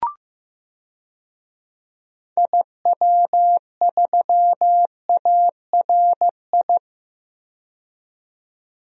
Morse code, heard as a keyed tone, about 15 wpm, 700 Hz. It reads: IW3ARI
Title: DXCC-100 Call Signs - 6 Characters - 15wpm